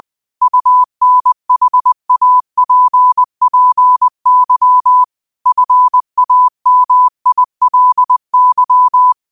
Morse code, heard as a keyed tone, about 20 wpm, 1000 Hz. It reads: UNHAPPY FAMILY
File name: qotdfiles / 20wpm